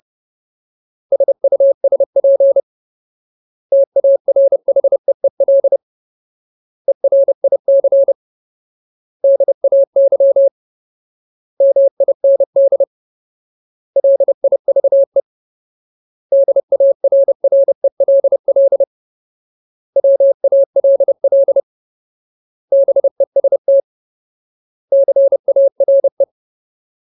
Morse code, heard as a keyed tone, about 30 wpm, 550 Hz. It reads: SUSP TARHEEL ERIC DAY MIND LIVE DARRELL WALL BEST CARE
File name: Ultimate Word List / 30WPM Ultimate